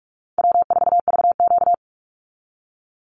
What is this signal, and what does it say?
Morse code, heard as a keyed tone, about 44 wpm, 700 Hz. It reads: W4VQ
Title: Call Signs Repeated 2x - 45wpm